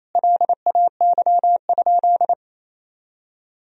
Morse code, heard as a keyed tone, about 28 wpm, 700 Hz. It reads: LAY?
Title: Sentences from Top 400 Words and Abbreviations 28wpm